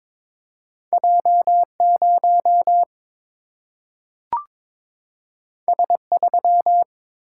Morse code, heard as a keyed tone, about 22 wpm, 700 Hz. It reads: J 0 S 3
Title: Single-Letter Single-Number 22wpm